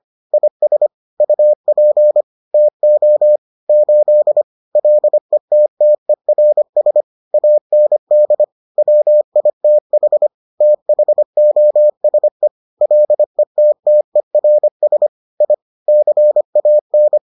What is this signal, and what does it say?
Morse code, heard as a keyed tone, about 25 wpm, 600 Hz. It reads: IS UP TO 8 LETTERS AND WITH THOSE LETTERS I CAN